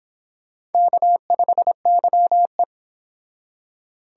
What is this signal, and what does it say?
Morse code, heard as a keyed tone, about 30 wpm, 700 Hz. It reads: K5YE